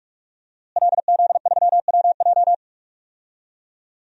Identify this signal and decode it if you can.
Morse code, heard as a keyed tone, about 44 wpm, 700 Hz. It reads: LZ3WJ
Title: DXCC-100 Call Signs - 5 Characters - Repeated 2x - 45wpm